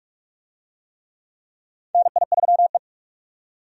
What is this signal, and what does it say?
Morse code, heard as a keyed tone, about 42 wpm, 700 Hz. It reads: NI3E
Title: USA Call Signs - 4 Characters - 45wpm